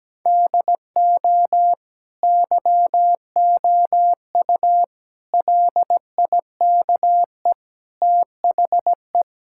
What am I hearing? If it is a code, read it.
Morse code, heard as a keyed tone, about 17 wpm, 700 Hz. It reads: DO YOU LIKE THE
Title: Sentences from Top 1000 Words - Review Part 6 - 17wpm